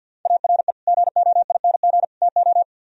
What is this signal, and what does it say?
Morse code, heard as a keyed tone, about 60 wpm, 700 Hz. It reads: ARE GOING TO
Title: Speed-Racing - Sets of 3 Words 50wpm